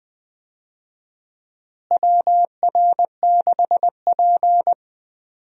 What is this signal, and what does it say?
Morse code, heard as a keyed tone, about 20 wpm, 700 Hz. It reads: WR6P
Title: USA Call Signs - 4 Characters - Repeated 2x - 20wpm